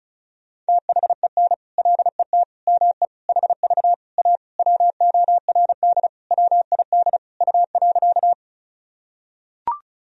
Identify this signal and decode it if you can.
Morse code, heard as a keyed tone, about 36 wpm, 700 Hz. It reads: THEN LET ME HV A WORD WID U.
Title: Sentences from Top 300 Words and Abbreviations 35wpm